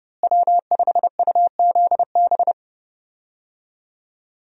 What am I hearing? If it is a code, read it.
Morse code, heard as a keyed tone, about 30 wpm, 700 Hz. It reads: W5UZB
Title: USA Call Signs - 5 Characters - Repeated 3x - 30wpm